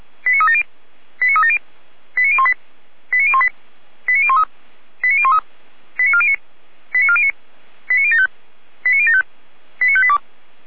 Five-tone selective call (ZVEI), heard as quick runs of five tones, tones of about 80 ms. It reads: 89370 89370 89018 89018 89013 89013 89409 89409 89075 89075 89582